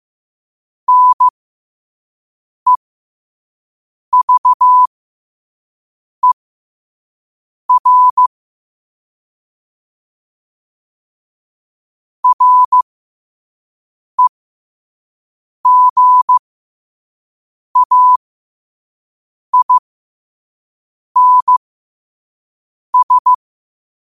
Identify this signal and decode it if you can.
Morse code, heard as a keyed tone, about 15 wpm, spaced as for 5 wpm, 1000 Hz. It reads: NEVER REGAINS
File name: qotdfiles / 5wpm